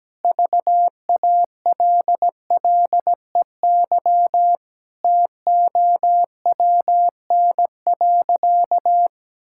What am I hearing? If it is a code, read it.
Morse code, heard as a keyed tone, about 17 wpm, 700 Hz. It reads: VALLEY TOWN.
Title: Pangram Challenge v2 - 17wpm